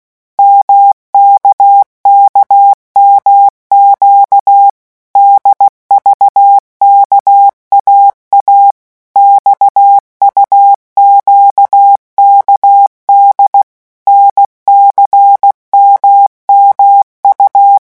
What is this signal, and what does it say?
Morse code, heard as a keyed tone, about 16 wpm, 800 Hz. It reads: MKKMQ DVKAA XUQKD NCMMU
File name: morse mp3 / revision3-vitesse 15 mots